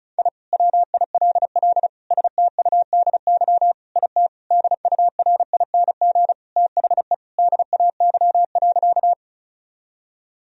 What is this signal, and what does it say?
Morse code, heard as a keyed tone, about 34 wpm, 700 Hz. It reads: I WILL STUDY IT DURING THE DAY.
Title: Sentences from Top 1000 Words - Review Part 2 - 35wpm